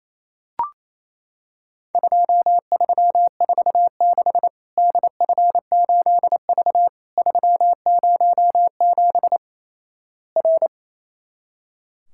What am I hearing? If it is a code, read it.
Morse code, heard as a keyed tone, about 28 wpm, 700 Hz. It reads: 2346 DF8V 307 R
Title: Worked All Europe DX Contest - QTC Exchange - 28wpm